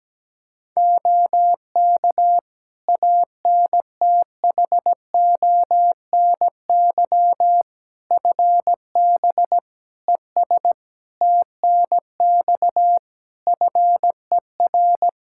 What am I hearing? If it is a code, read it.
Morse code, heard as a keyed tone, about 17 wpm, 700 Hz. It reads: OK ANTHONY FB ES TNX FER
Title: Practice QSO - Exchange 2 - Info - 17wpm